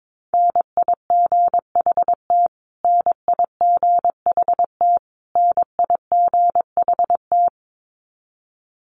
Morse code, heard as a keyed tone, about 22 wpm, 700 Hz. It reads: NIGHT NIGHT NIGHT
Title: Top 500 Words Repeated 22wpm